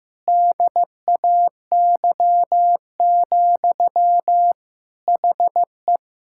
Morse code, heard as a keyed tone, about 15 wpm, 700 Hz. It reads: DAY, HE
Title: Sentences from Top 1000 Words - Review Part 6 - 15wpm